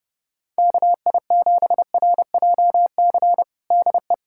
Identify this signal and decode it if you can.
Morse code, heard as a keyed tone, about 30 wpm, 700 Hz. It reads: KI7RJC DE
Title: USA Callsign DE Callsign - 6 Characters - 30wpm